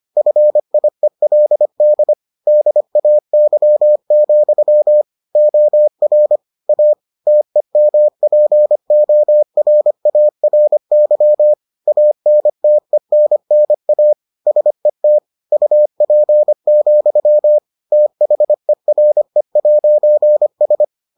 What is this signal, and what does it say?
Morse code, heard as a keyed tone, about 25 wpm, 600 Hz. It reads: FIELD DAY, OR A TEMPORARY ANTENNA SET UP, THERE'S